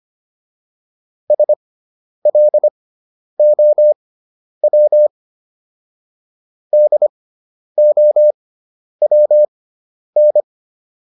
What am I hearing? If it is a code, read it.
Morse code, heard as a keyed tone, about 25 wpm, spaced as for 10 wpm, 600 Hz. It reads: SLOW DOWN